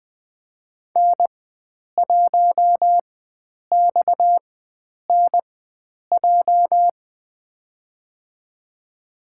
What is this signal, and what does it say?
Morse code, heard as a keyed tone, about 20 wpm, spaced as for 9 wpm, 700 Hz. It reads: N1XNJ
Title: USA Call Signs - 4x Character Spacing - 20wpm